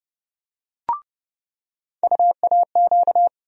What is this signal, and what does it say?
Morse code, heard as a keyed tone, about 30 wpm, 700 Hz. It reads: UAQ
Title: USA Call Sign Postfixes - 3 Letters 30wpm